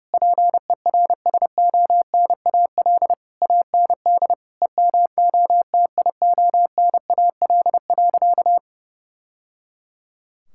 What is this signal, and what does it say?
Morse code, heard as a keyed tone, about 32 wpm, 700 Hz. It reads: PERSONAL AND EMOTIONAL.
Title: Sentences from Top 1000 Words - Review Part 5 - 30wpm